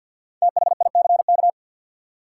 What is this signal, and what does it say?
Morse code, heard as a keyed tone, about 53 wpm, 700 Hz. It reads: THICK
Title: Five Letter Words - 50wpm